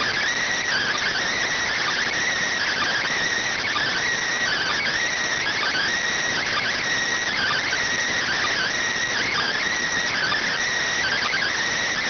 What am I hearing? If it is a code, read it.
Emergency Alert System (SAME) tones (header bursts), heard as harsh buzzing bursts. It is heard from 0.6 to 11.5 s.